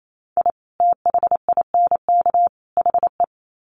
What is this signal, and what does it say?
Morse code, heard as a keyed tone, about 28 wpm, 700 Hz. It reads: I THINK HE
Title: Sets of 3 Words 28wpm